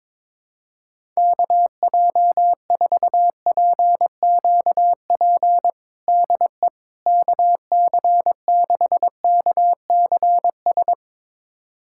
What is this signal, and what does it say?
Morse code, heard as a keyed tone, about 22 wpm, 700 Hz. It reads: KJ4PQP DE KC6KCS
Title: USA Callsign DE Callsign - 6 Characters - 22wpm